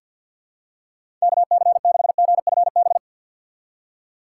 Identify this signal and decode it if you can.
Morse code, heard as a keyed tone, about 51 wpm, 700 Hz. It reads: KC6GFB
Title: USA Call Signs - 6 Characters - 3x Character Spacing - 50wpm